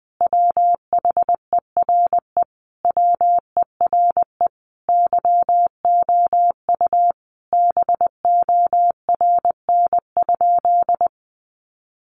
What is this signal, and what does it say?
Morse code, heard as a keyed tone, about 20 wpm, 700 Hz. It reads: WHERE WERE YOU BORN?
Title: Sentences from Top 1000 Words 20wpm